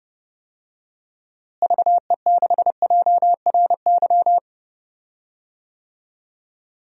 Morse code, heard as a keyed tone, about 32 wpm, 700 Hz. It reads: VE6JRY
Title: DXCC-100 Call Signs - 6 Characters - Repeated 3x - 30wpm